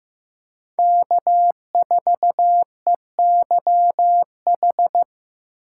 Morse code, heard as a keyed tone, about 15 wpm, 700 Hz. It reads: K4EYH
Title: USA Call Signs - 5 Characters - 6x Character Spacing - 15wpm